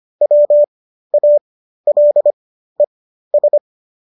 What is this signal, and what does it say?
Morse code, heard as a keyed tone, about 25 wpm, spaced as for 13 wpm, 590 Hz. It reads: WALES